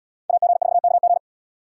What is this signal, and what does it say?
Morse code, heard as a keyed tone, about 77 wpm, 700 Hz. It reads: AG4KG